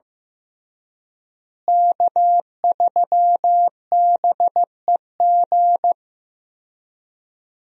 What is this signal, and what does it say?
Morse code, heard as a keyed tone, about 15 wpm, 700 Hz. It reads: K3BEG
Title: USA Call Signs - 5 Characters - Repeated 2x - 15wpm